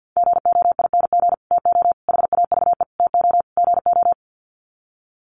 Morse code, heard as a keyed tone, about 50 wpm, 700 Hz. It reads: GOING TO HAVE TO GO
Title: Sets of 5 Words Encore 50wpm